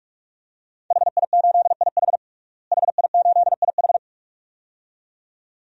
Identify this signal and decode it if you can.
Morse code, heard as a keyed tone, about 46 wpm, 700 Hz. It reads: HI8IH HI8IH
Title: DXCC-100 Call Signs - 5 Characters - Repeated 2x - 45wpm